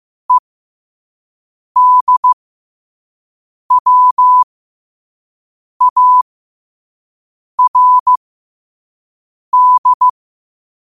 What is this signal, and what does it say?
Morse code, heard as a keyed tone, about 15 wpm, spaced as for 5 wpm, 1000 Hz. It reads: EDWARD
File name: qotdfiles / 5wpm